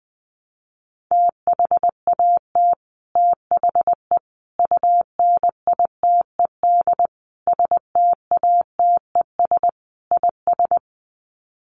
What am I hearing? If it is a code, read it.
Morse code, heard as a keyed tone, about 20 wpm, 700 Hz. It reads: THAT THE UNITED STATES IS
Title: Sets of 5 Words Encore 20wpm